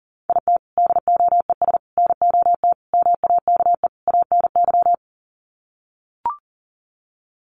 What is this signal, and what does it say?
Morse code, heard as a keyed tone, about 40 wpm, 700 Hz. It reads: IT DOES NOT MAKE ANY
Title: Sets of 5 Words Encore2 40wpm